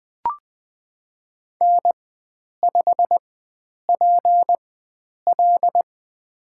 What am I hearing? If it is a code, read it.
Morse code, heard as a keyed tone, about 20 wpm, spaced as for 9 wpm, 700 Hz. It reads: N5PL
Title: USA Call Signs - 4 Characters - 4x Character Spacing - 20wpm